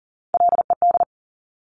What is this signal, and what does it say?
Morse code, heard as a keyed tone, about 41 wpm, 700 Hz. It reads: LED
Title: Top Three Letter Combinations in Words 40wpm